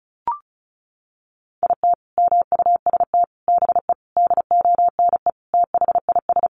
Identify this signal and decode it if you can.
Morse code, heard as a keyed tone, about 34 wpm, 700 Hz. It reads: IT MUST BE DONE THIS
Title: Sentences from Top 1500 Words 35wpm